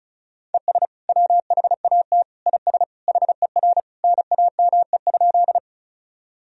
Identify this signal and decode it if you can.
Morse code, heard as a keyed tone, about 36 wpm, 700 Hz. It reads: ES WHAT IS HER NAME?
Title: Sentences from Top 200 Words and Abbreviations 35wpm